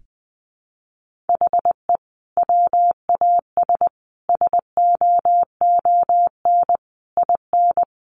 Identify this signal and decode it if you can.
Morse code, heard as a keyed tone, about 20 wpm, 700 Hz. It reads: HE WAS SOON IN
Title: Sentences from Top 1200 Words 20wpm